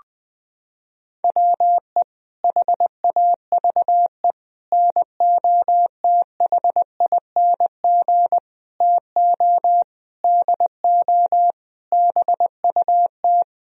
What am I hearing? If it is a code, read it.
Morse code, heard as a keyed tone, about 20 wpm, 700 Hz. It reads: WE HAVE NOTHING TO DO BUT
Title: Sentences from Top 1000 Words - Review Part 1 - 20wpm